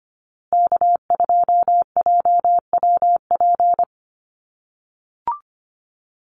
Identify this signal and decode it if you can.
Morse code, heard as a keyed tone, about 25 wpm, 700 Hz. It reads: K2JWP
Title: USA Call Signs Repeated 3x 25wpm